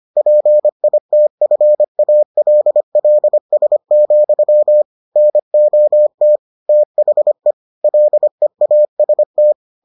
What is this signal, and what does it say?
Morse code, heard as a keyed tone, about 25 wpm, 600 Hz. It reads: PITFALLS, NOT THE LEAST